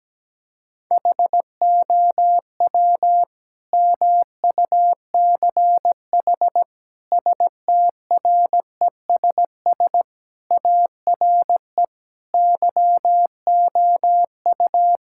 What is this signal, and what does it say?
Morse code, heard as a keyed tone, about 17 wpm, 700 Hz. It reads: HOW MUCH STRESS ARE YOU
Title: Sentences from Top 1000 Words - Review Part 6 - 17wpm